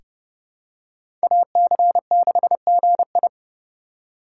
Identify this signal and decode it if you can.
Morse code, heard as a keyed tone, about 30 wpm, 700 Hz. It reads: AC6GI
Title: USA Call Signs - 5 Characters - 4x Character Spacing - 30wpm